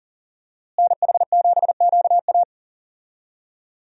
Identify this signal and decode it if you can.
Morse code, heard as a keyed tone, about 40 wpm, 700 Hz. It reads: NH7QA
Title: USA Call Signs - 5 Characters - Repeated 2x - 40wpm